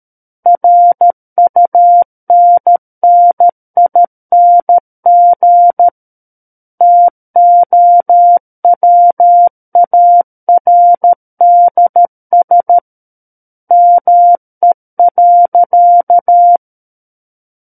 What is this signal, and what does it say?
Morse code, heard as a keyed tone, about 13 wpm, 700 Hz. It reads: RUNNING TOWARDS ME.
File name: Book One / 04-Chapter 4 - 13 WPM